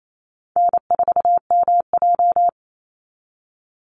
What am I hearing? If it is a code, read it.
Morse code, heard as a keyed tone, about 28 wpm, 700 Hz. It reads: N4MJ
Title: NA SOTA Chaser Callsigns 28wpm